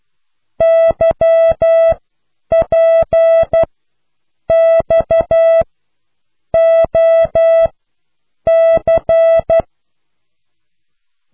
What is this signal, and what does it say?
Morse code, heard as a keyed tone, about 12 wpm, spaced as for 7 wpm, 660 Hz. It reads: YPXOC